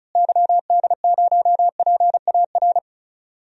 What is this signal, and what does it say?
Morse code, heard as a keyed tone, about 34 wpm, 700 Hz. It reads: YD0PAR
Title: DXCC-100 Call Signs - 6 Characters - Repeated 2x - 35wpm